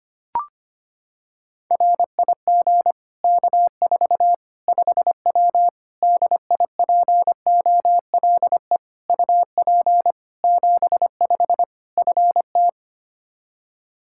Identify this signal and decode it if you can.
Morse code, heard as a keyed tone, about 25 wpm, 700 Hz. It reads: RIG K4 5W DIPOLE UP 75 FT
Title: Practice QSO - Exchange 2 - Info - 25wpm